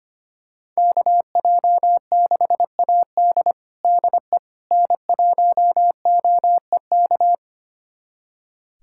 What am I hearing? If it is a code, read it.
Morse code, heard as a keyed tone, about 25 wpm, 700 Hz. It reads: KJ6AD DE N1OEK
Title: USA Callsign DE Callsign - 5 Characters - 25wpm